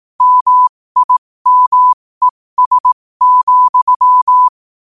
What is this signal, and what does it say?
Morse code, heard as a keyed tone, about 18 wpm, spaced as for 15 wpm, 1000 Hz. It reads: MIMES,
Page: New quotes every day in morse code at 15 Words per minute.